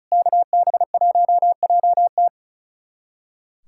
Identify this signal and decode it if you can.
Morse code, heard as a keyed tone, about 36 wpm, 700 Hz. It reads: KB1JT
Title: USA Callsign DE Callsign - 5 Characters - 35wpm